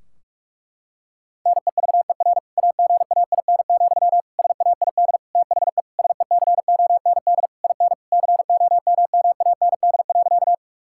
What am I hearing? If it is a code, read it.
Morse code, heard as a keyed tone, about 43 wpm, 700 Hz. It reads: NEVER AGAIN, SAID THE SECOND IN COMMAND.
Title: Sentences from Top 1600 Words 45wpm